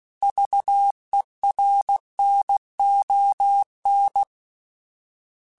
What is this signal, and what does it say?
Morse code, heard as a keyed tone, about 16 wpm, 800 Hz. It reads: VERNON
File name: fromSMs / FromSMs mp3s / Cities@15